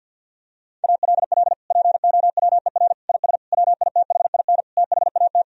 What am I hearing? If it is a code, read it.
Morse code, heard as a keyed tone, about 52 wpm, 700 Hz. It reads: ALL POWER IS WITHIN THAT
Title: Sentences from Top 1000 Words - Review Part 4 - 50wpm